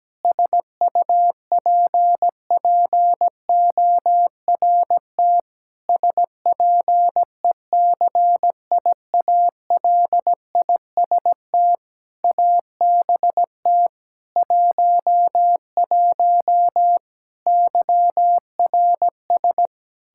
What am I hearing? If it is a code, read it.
Morse code, heard as a keyed tone, about 17 wpm, 700 Hz. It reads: SUPPORT SPECIALIST ABT 11 YRS
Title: Practice QSO - Exchange 3 - More Info - 17wpm